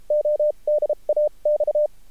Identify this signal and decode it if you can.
Morse code, heard as a keyed tone, about 33 wpm, spaced as for 28 wpm, 600 Hz. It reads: ODA X
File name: archived / CRSnet